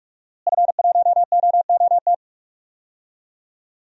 Morse code, heard as a keyed tone, about 42 wpm, 700 Hz. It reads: F1OOT
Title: DXCC-100 Call Signs - 5 Characters - Repeated 2x - 45wpm